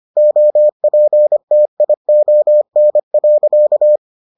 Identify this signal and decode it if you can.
Morse code, heard as a keyed tone, about 25 wpm, 600 Hz. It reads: OPTION.